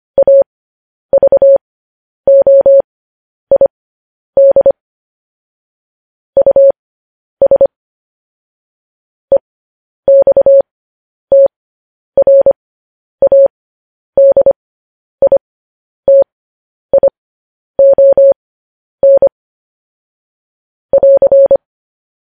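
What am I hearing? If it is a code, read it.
Morse code, heard as a keyed tone, about 25 wpm, spaced as for 10 wpm, 550 Hz. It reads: AVOID US EXTRADITION <AR>
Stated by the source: Morse code practice sending news headlines at 10 wpm.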